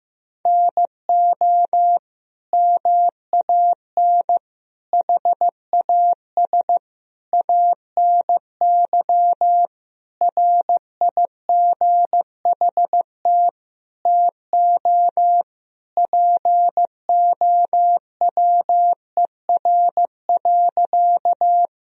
Morse code, heard as a keyed tone, about 15 wpm, 700 Hz. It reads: NO MAN HAS ANY RIGHT TO POWER.
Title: Sentences from Top 1000 Words - Review Part 2 - 15wpm